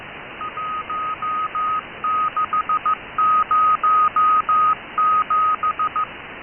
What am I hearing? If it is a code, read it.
Morse code, heard as a keyed tone, about 15 wpm, 1250 Hz. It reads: 1607
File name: Shindaisat cw